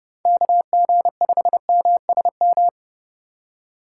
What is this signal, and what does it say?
Morse code, heard as a keyed tone, about 30 wpm, 700 Hz. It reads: KG5MSM
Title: USA Call Signs - 6 Characters - 2x Character Spacing - 30wpm